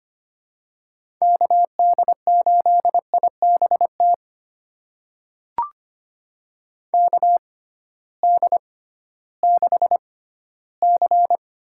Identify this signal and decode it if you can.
Morse code, heard as a keyed tone, about 25 wpm, spaced as for 15 wpm, 700 Hz. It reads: KD8IBT KD6C
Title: USA Call Signs - 6 Characters - 6x Character Spacing - 25wpm